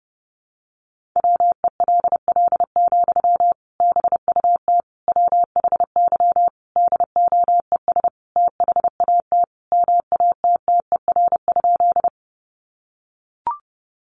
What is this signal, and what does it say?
Morse code, heard as a keyed tone, about 30 wpm, 700 Hz. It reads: WELL, BUT WHY DOES THAT MATTER?
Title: Sentences from Top 700 Words 30wpm